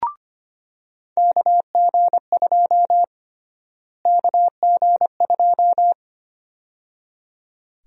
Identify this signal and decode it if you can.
Morse code, heard as a keyed tone, about 25 wpm, 700 Hz. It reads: KG2 KG2
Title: USA Call Sign Prefixes - Repeated 2x - 25wpm